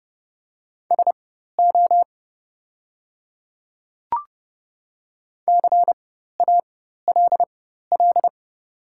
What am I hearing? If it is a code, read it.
Morse code, heard as a keyed tone, about 30 wpm, spaced as for 14 wpm, 700 Hz. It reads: SO CALL
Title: 4x Character Spacing - Top 200 Words - 30wpm